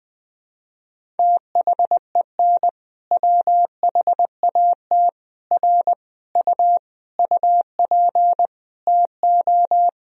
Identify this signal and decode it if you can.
Morse code, heard as a keyed tone, about 21 wpm, 700 Hz. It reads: THEN WHAT R U UP TO
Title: Sentences from Top 100 Words and Abbreviations 20wpm